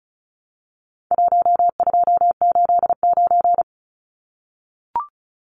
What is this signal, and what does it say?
Morse code, heard as a keyed tone, about 34 wpm, 700 Hz. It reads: 1289
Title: Numbers - Contest Style - 4-Digit 35wpm